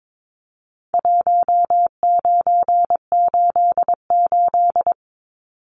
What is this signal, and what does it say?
Morse code, heard as a keyed tone, about 22 wpm, 700 Hz. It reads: 1988
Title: Numbers - Contest Style - 4-Digit 22wpm